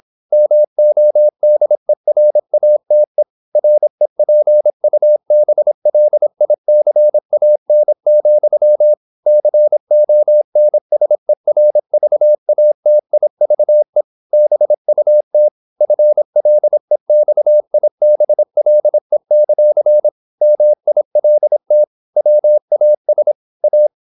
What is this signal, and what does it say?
Morse code, heard as a keyed tone, about 26 wpm, 600 Hz. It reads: MODERATE REPUBLICAN, CONSERVATIVE BUT FLEXIBLE; MILT WAS A
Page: Never 26wpm | CW med Gnister